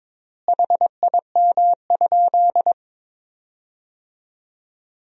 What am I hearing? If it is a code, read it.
Morse code, heard as a keyed tone, about 22 wpm, 700 Hz. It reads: HIM?
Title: 3x Word-Spacing - Sentences from Top 100 Words - 22wpm